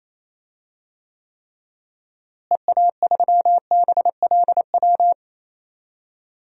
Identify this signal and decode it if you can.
Morse code, heard as a keyed tone, about 28 wpm, 700 Hz. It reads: EA3BLW
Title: DXCC-100 Call Signs - 6 Characters - Repeated 3x - 28wpm